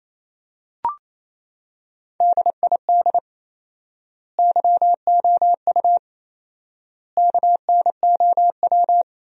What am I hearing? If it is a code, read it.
Morse code, heard as a keyed tone, about 28 wpm, 700 Hz. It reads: DID YOU KNOW
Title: 4x Word-Spacing - Sentences from Top 100 Words - 28wpm